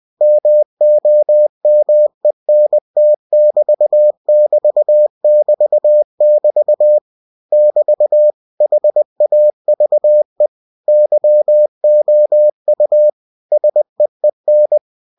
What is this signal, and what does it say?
Morse code, heard as a keyed tone, about 20 wpm, 600 Hz. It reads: MOMENT<BT><BT><BT><BT> <BT> HAVE YOU SEEN
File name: cw-books / winnie-the-pooh / winnie-20wpm-page-0097